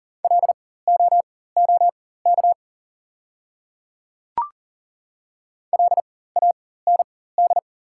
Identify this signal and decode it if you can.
Morse code, heard as a keyed tone, about 38 wpm, spaced as for 18 wpm, 690 Hz. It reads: LOOK LAND
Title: 4x Character Spacing - Top 200 Words - 40wpm